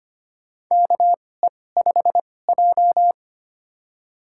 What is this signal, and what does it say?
Morse code, heard as a keyed tone, about 25 wpm, spaced as for 18 wpm, 700 Hz. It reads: KE5J
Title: USA Call Signs - 4 Characters - 2x Character Spacing - 25wpm